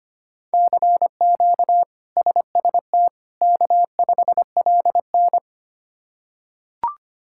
Morse code, heard as a keyed tone, about 25 wpm, 700 Hz. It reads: CQ SST K5LN
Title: K1USN SST QSO Practice - 25wpm